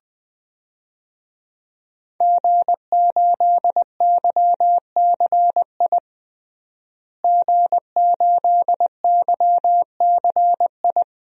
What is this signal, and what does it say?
Morse code, heard as a keyed tone, about 21 wpm, 700 Hz. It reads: G8YCI G8YCI
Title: DXCC-100 Call Signs - 5 Characters - Repeated 2x - 20wpm